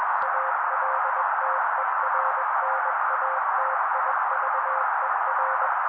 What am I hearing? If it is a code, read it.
Morse code, heard as a keyed tone, about 21 wpm, 530 Hz. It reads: ALTERNATIVER
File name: daily / 20wpm noise